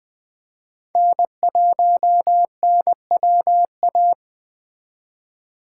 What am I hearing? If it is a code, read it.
Morse code, heard as a keyed tone, about 21 wpm, 700 Hz. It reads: N1NWA